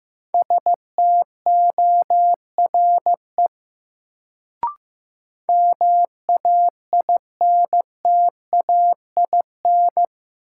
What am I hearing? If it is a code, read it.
Morse code, heard as a keyed tone, about 15 wpm, 700 Hz. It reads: STORE MAINTAIN
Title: Sentences from Top 1000 Words - Review Part 5 - New Word Drill - 15wpm